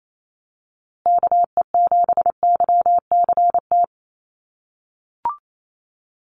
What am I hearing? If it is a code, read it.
Morse code, heard as a keyed tone, about 28 wpm, 700 Hz. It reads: KE7YCT
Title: USA Call Signs Repeated 2x 28wpm